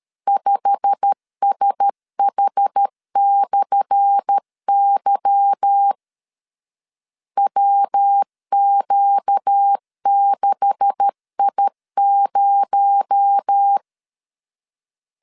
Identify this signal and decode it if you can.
Morse code, heard as a keyed tone, about 13 wpm, 800 Hz. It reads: 5SH/Y WQ6I0
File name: LEZIONI / 10 SIGNS / 10A SLASH 12wpm 2314